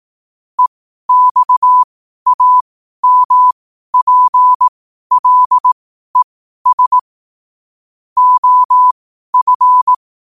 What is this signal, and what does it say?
Morse code, heard as a keyed tone, about 18 wpm, spaced as for 12 wpm, 1000 Hz. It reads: EXAMPLES OF
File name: qotdfiles / 12wpm